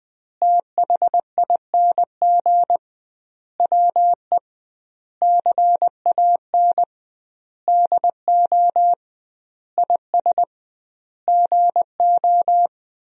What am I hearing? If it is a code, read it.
Morse code, heard as a keyed tone, about 21 wpm, 700 Hz. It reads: THING WE CAN DO IS GO
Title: 2x Word-Spacing - Sentences from Top 200 Words - 20wpm